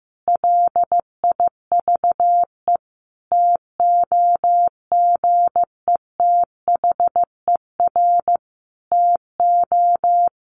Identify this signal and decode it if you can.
Morse code, heard as a keyed tone, about 15 wpm, 700 Hz. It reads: LIVE TOGETHER TO
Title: Sentences from Top 1100 Words 15wpm